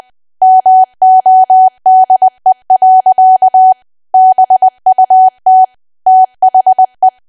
Morse code, heard as a keyed tone, about 20 wpm, 750 Hz. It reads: MODE. BUT THE
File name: cwa / W1AW-Level3 / 20WPM